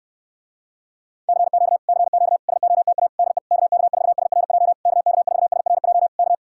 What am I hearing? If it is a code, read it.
Morse code, heard as a keyed tone, about 67 wpm, 700 Hz. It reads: CQ CQ SOTA DE KK4SRJ KK4SRJ K